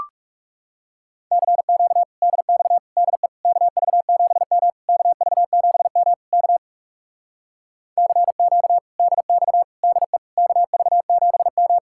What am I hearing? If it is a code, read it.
Morse code, heard as a keyed tone, about 42 wpm, 700 Hz. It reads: CQ DX DE KV7M KV7M K CQ DX DE KV7M